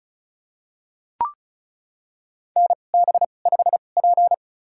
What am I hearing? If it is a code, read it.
Morse code, heard as a keyed tone, about 36 wpm, spaced as for 25 wpm, 700 Hz. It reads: NB5P
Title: USA Call Signs - 4 Characters - 2x Character Spacing - 35wpm